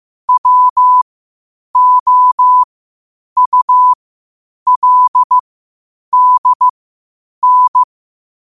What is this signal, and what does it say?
Morse code, heard as a keyed tone, about 15 wpm, spaced as for 8 wpm, 1000 Hz. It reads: WOULDN
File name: qotdfiles / 8wpm